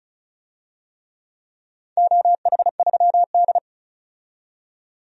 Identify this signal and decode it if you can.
Morse code, heard as a keyed tone, about 36 wpm, 700 Hz. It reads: OH3D
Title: DXCC-100 Call Signs - 4 Characters - 35wpm